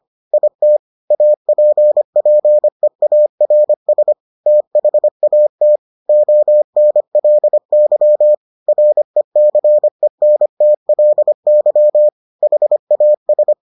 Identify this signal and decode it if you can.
Morse code, heard as a keyed tone, about 25 wpm, 600 Hz. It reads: IT APPEARS THAT ONLY RECENTLY HAS